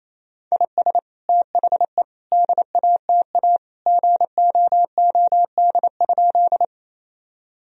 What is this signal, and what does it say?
Morse code, heard as a keyed tone, about 28 wpm, 700 Hz. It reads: IS THE DATA GOOD?
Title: Sentences from Top 1000 Words - Review Part 3 - 28wpm